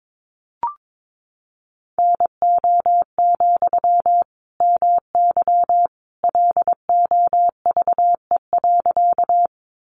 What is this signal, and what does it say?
Morse code, heard as a keyed tone, about 22 wpm, 700 Hz. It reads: NO, MY LOVE.
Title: Sentences from Top 400 Words 22wpm